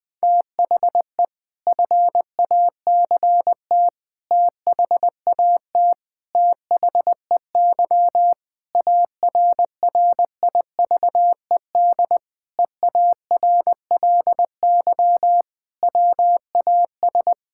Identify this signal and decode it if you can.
Morse code, heard as a keyed tone, about 21 wpm, 700 Hz. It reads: THE FACT THAT THEY ARRIVED EARLY WAS
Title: Sentences with 4-Word Phrases - Spoken Beginning - 20wpm